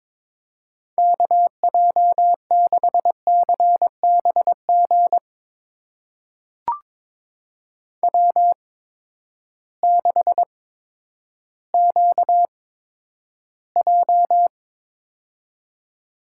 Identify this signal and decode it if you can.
Morse code, heard as a keyed tone, about 22 wpm, spaced as for 11 wpm, 700 Hz. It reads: KJ6CBG W6QJ
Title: USA Call Signs - 8x Character Spacing - 22wpm